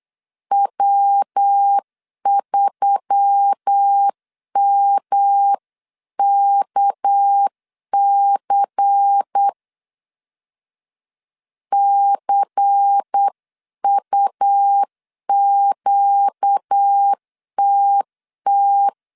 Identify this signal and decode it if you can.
Morse code, heard as a keyed tone, about 8 wpm, 800 Hz. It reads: W3MKC CUQTT
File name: CW / DET / 1 6 1500C S C 8wpm 5219